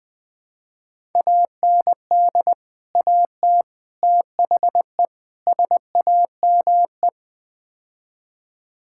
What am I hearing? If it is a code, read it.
Morse code, heard as a keyed tone, about 20 wpm, 700 Hz. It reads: AND AT THE SAME
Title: Build A Phrase - 5-Word Phrases - 20wpm